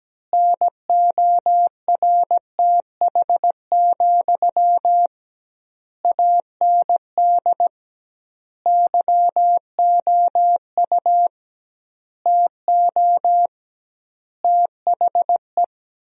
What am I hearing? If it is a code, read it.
Morse code, heard as a keyed tone, about 17 wpm, 700 Hz. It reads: NORTH, AND YOU TO THE
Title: 2x Word-Spacing - Sentences from Top 400 Words - 17wpm